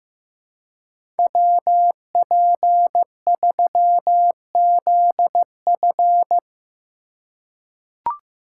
Morse code, heard as a keyed tone, about 15 wpm, 700 Hz. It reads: WP3ZF
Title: USA Call Signs - 5 Characters - 2x Character Spacing - 15wpm